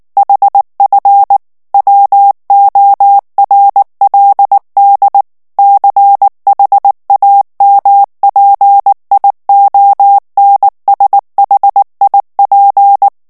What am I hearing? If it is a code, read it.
Morse code, heard as a keyed tone, about 19 wpm, 800 Hz. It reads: HF WORLD CHAMPIONSHIP
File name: morse mp3 / ALLEMAND texte clair 2 vitesse 18 mots minute